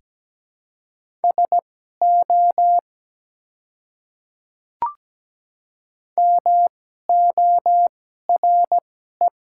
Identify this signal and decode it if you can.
Morse code, heard as a keyed tone, about 17 wpm, spaced as for 12 wpm, 700 Hz. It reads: SO MORE
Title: 2x Character Spacing - Top 100 Words - 17wpm